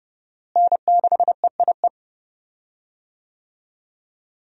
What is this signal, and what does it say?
Morse code, heard as a keyed tone, about 30 wpm, 700 Hz. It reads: N6EIE
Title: USA Call Signs - 5 Characters - Repeated 3x - 30wpm